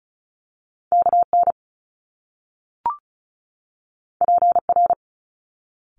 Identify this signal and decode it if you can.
Morse code, heard as a keyed tone, about 33 wpm, 700 Hz. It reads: KN PR
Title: Top Two Letter Combinations in Words 35wpm